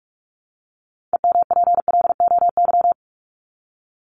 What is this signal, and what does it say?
Morse code, heard as a keyed tone, about 44 wpm, 700 Hz. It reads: EMPLOY
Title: ED Suffix - Top 100 Words with Base Words 45wpm